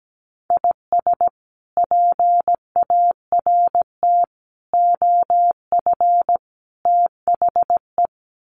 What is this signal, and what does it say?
Morse code, heard as a keyed tone, about 17 wpm, 700 Hz. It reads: IS PART OF THE
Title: Sets of 4 Words 17wpm